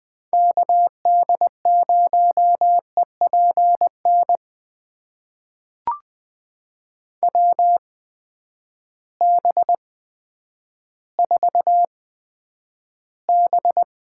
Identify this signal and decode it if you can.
Morse code, heard as a keyed tone, about 20 wpm, spaced as for 10 wpm, 700 Hz. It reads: KD0EPN WB4B
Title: USA Call Signs - 8x Character Spacing - 20wpm